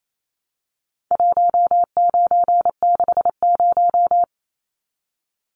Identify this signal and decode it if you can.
Morse code, heard as a keyed tone, about 28 wpm, 700 Hz. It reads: 1960
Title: Numbers - 4-Digit 28wpm